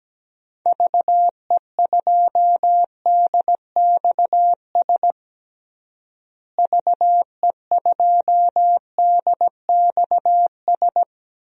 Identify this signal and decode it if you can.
Morse code, heard as a keyed tone, about 17 wpm, 700 Hz. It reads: VE2DXS VE2DXS
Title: DXCC-100 Call Signs - 6 Characters - Repeated 3x - 17wpm